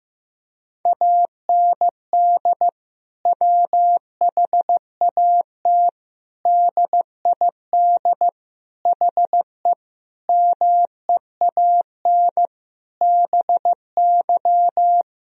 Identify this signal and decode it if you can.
Morse code, heard as a keyed tone, about 15 wpm, 700 Hz. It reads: AND WHAT DID HE MEAN BY
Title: Sentences from Top 1000 Words - Review Part 3 - 15wpm